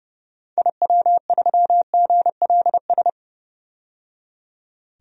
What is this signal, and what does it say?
Morse code, heard as a keyed tone, about 31 wpm, 700 Hz. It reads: IW3GLS
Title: DXCC-100 Call Signs - 6 Characters - 30wpm